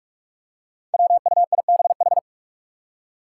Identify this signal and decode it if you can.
Morse code, heard as a keyed tone, about 44 wpm, 700 Hz. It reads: WUIBH
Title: Five Challenging Letter-Number Combinations 45wpm